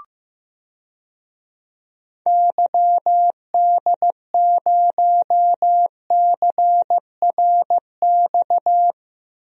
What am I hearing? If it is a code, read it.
Morse code, heard as a keyed tone, about 15 wpm, 700 Hz. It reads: YD0CRX
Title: DXCC-100 Call Signs - 6 Characters - 15wpm